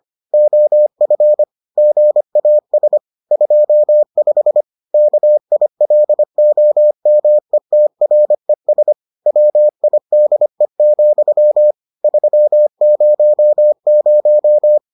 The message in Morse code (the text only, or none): OF GAS 25 KILOMETRES WIDE, 300